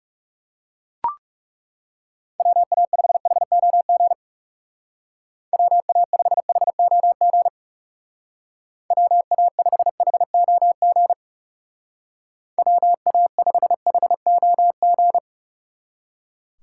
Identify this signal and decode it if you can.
Morse code, heard as a keyed tone, about 35 wpm, 700 Hz. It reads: WA5HOG WA5HOG WA5HOG WA5HOG
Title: Speed Racing - USA 2x3 Call Signs 30wpm